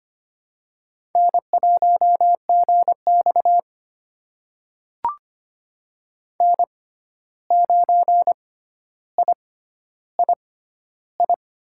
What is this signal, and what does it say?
Morse code, heard as a keyed tone, about 25 wpm, spaced as for 12 wpm, 700 Hz. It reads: N1GX N9III
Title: USA Call Signs - 6x Character Spacing - 25wpm